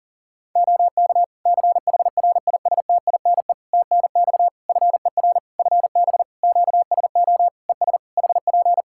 Morse code, heard as a keyed tone, about 42 wpm, 700 Hz. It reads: OK CHRISTINE TNX FER FB QSO ES HP
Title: Practice QSO - Exchange 4 - Ending - 40wpm